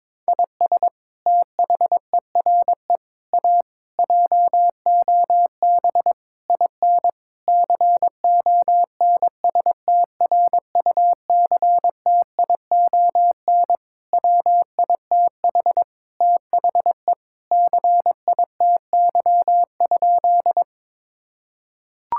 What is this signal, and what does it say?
Morse code, heard as a keyed tone, about 22 wpm, 700 Hz. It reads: IS THERE A JOB IN CONSTRUCTION WITH THE CITY?
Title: Sentences from Top 1600 Words 22wpm